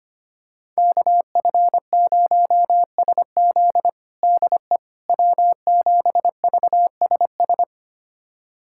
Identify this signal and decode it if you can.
Morse code, heard as a keyed tone, about 25 wpm, 700 Hz. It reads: KF0SZ DE W7VSS
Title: USA Callsign DE Callsign - 5 Characters - 25wpm